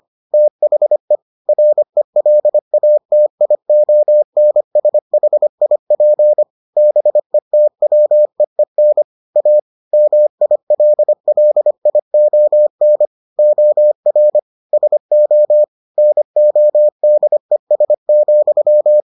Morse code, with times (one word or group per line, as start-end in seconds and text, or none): 0.3-1.2 THE
1.5-6.4 RELATIONSHIP
6.8-9.0 BETWEEN
9.4-9.6 A
9.9-13.1 MILLION
13.4-14.4 OR
14.7-15.6 SO
16.0-19.0 NODES,